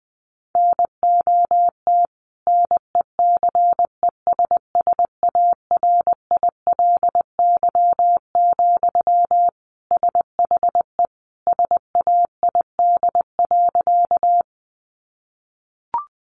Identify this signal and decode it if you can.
Morse code, heard as a keyed tone, about 20 wpm, 700 Hz. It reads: NOT NECESSARILY, SHE SAID.
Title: Sentences from Top 1400 Words 20wpm